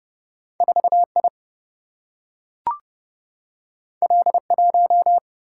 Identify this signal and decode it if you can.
Morse code, heard as a keyed tone, about 31 wpm, 700 Hz. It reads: 4 I L 1
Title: Single-Letter Single-Number 30wpm